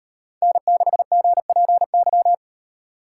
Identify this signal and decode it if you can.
Morse code, heard as a keyed tone, about 38 wpm, 700 Hz. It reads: N6GPY